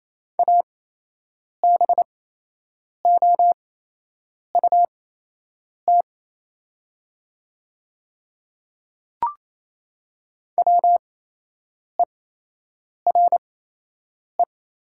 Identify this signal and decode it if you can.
Morse code, heard as a keyed tone, about 27 wpm, spaced as for 8 wpm, 690 Hz. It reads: ABOUT WERE
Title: 8x Character Spacing - Top 100 Words - 28wpm